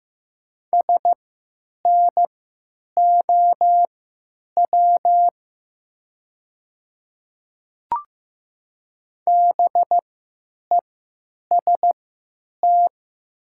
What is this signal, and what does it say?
Morse code, heard as a keyed tone, about 15 wpm, spaced as for 8 wpm, 700 Hz. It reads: SNOW BEST
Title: 3x Character Spacing - Top 500 Words - 15wpm